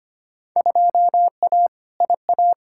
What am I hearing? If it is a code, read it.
Morse code, heard as a keyed tone, about 25 wpm, 700 Hz. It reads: 2A IA
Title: Field Day Exchanges Encore - 25wpm